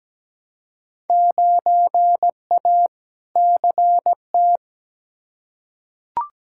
Morse code, heard as a keyed tone, about 17 wpm, 700 Hz. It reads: 9A CT
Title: Field Day Exchanges Encore - 17wpm